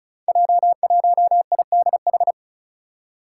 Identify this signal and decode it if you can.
Morse code, heard as a keyed tone, about 36 wpm, 700 Hz. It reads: J1IDH
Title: Five Challenging Letter-Number Combinations 35wpm